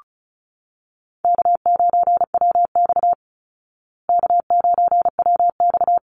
Morse code, heard as a keyed tone, about 34 wpm, 700 Hz. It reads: K9WX K9WX
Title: Easy Call Signs Repeated 3x - 35wpm